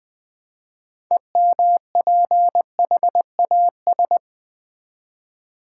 Morse code, heard as a keyed tone, about 20 wpm, 700 Hz. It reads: EMPHAS
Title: Build A Word - 3 Letters At A Time - 8-Letter Words - 20wpm